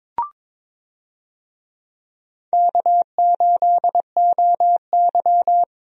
Morse code, heard as a keyed tone, about 22 wpm, 700 Hz. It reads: K8OY
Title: USA Call Signs - 4 Characters - Repeated 3x - 22wpm